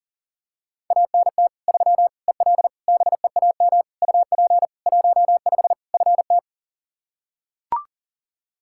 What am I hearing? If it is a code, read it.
Morse code, heard as a keyed tone, about 40 wpm, 700 Hz. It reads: ANT 3 EL BEAM UP 15 FT
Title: Practice QSO - Exchange 2 - Info - 40wpm